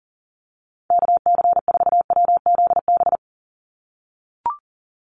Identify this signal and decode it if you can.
Morse code, heard as a keyed tone, about 40 wpm, 700 Hz. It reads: KC4WZB
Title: NA SOTA Activator Callsigns 40wpm